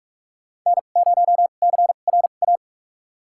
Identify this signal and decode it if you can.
Morse code, heard as a keyed tone, about 46 wpm, spaced as for 33 wpm, 700 Hz. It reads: N0CRA
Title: USA Call Signs - 2x Character Spacing - 45wpm